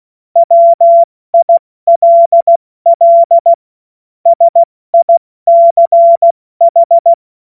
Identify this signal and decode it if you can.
Morse code, heard as a keyed tone, about 16 wpm, 680 Hz. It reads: WILL SICH
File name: Morsekurs / Witze / Witz 02 April 2024 70 F80